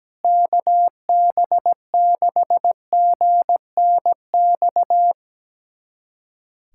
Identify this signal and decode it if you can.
Morse code, heard as a keyed tone, about 17 wpm, 700 Hz. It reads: KB6GNX
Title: USA Callsign DE Callsign 17wpm